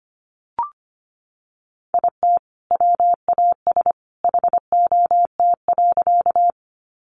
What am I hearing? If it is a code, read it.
Morse code, heard as a keyed tone, about 25 wpm, 700 Hz. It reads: IT WAS HOT.
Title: Sentences from Top 500 Words 25wpm